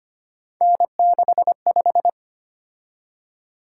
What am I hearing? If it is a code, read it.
Morse code, heard as a keyed tone, about 25 wpm, 700 Hz. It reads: N65
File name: Three Letter-Number Combinations - 25wpm